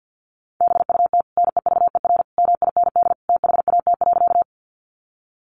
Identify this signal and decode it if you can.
Morse code, heard as a keyed tone, about 50 wpm, 700 Hz. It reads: BUT NEVER MIND THAT.
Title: Sentences from Top 500 Words 50wpm